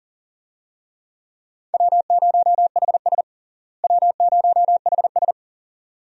Morse code, heard as a keyed tone, about 37 wpm, 700 Hz. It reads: W0HS W0HS
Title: USA Call Signs - 4 Characters - Repeated 2x - 40wpm